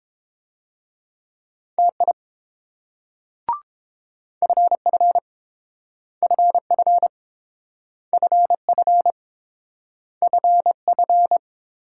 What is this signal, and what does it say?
Morse code, heard as a keyed tone, about 28 wpm, 700 Hz. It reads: TI FF FF FF FF